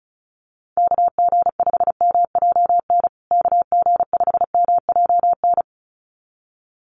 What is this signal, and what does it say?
Morse code, heard as a keyed tone, about 34 wpm, 700 Hz. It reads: KG5MJN KG5MJN
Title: USA Call Signs Repeated 2x 35wpm